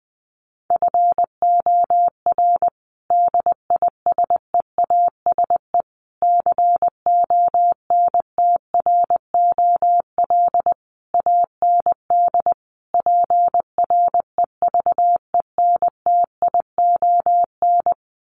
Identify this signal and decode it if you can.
Morse code, heard as a keyed tone, about 20 wpm, 700 Hz. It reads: FOR DISEASE CONTROL AND PREVENTION
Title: Sets of 5 Words Encore2 20wpm